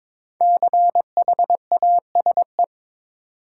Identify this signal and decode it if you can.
Morse code, heard as a keyed tone, about 22 wpm, 700 Hz. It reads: CHASE
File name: 5 Letter Words - First 3 Letters Given - 22wpm